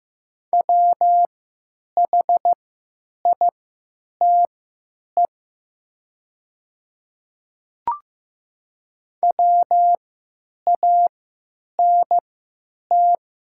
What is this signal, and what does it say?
Morse code, heard as a keyed tone, about 15 wpm, spaced as for 8 wpm, 700 Hz. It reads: WHITE WANT
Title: 3x Character Spacing - Top 300 Words - 15wpm